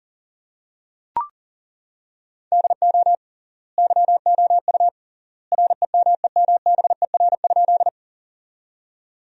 Morse code, heard as a keyed tone, about 37 wpm, 700 Hz. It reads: DO YOU REMEMBER?
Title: 3x Word-Spacing - Sentences from Top 400 Words - 40wpm